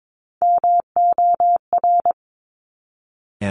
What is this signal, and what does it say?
Morse code, heard as a keyed tone, about 22 wpm, 700 Hz. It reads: MOR
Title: Top Three Letter Combinations in Words 22wpm